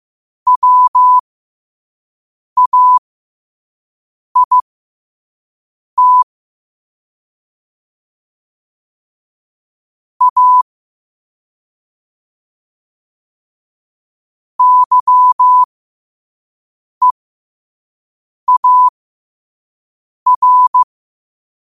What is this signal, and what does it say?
Morse code, heard as a keyed tone, about 15 wpm, spaced as for 5 wpm, 1000 Hz. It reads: WAIT A YEAR
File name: qotdfiles / 5wpm